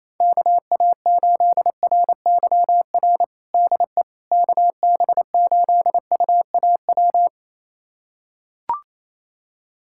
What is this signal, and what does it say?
Morse code, heard as a keyed tone, about 28 wpm, 700 Hz. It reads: KA8RYR DE KB8UAW
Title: USA Callsign DE Callsign 28wpm